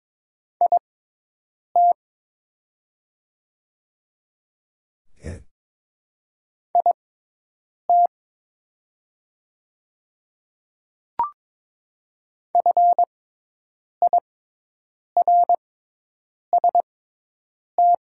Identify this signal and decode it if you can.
Morse code, heard as a keyed tone, about 22 wpm, spaced as for 8 wpm, 700 Hz. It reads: IT IT FIRST
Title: 6x Character Spacing - Top 100 Words - 22wpm